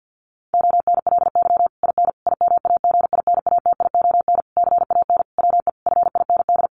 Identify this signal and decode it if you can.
Morse code, heard as a keyed tone, about 50 wpm, 700 Hz. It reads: ONLY IN IMAGINATION CAN WE FIND
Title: Sentences from Top 1300 Words 50wpm